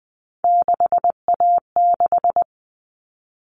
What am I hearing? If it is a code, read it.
Morse code, heard as a keyed tone, about 20 wpm, 700 Hz. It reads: 6A6
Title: All Cut Numbers - 3-Digit 20wpm